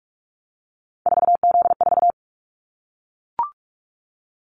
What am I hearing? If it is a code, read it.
Morse code, heard as a keyed tone, about 44 wpm, 700 Hz. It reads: 4Z4
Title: Callsign Prefixes for Top 50 DXCC - 45wpm